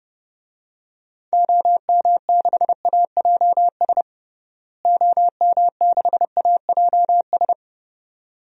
Morse code, heard as a keyed tone, about 32 wpm, 700 Hz. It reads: OM6AJS OM6AJS
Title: DXCC-100 Call Signs - 6 Characters - Repeated 2x - 30wpm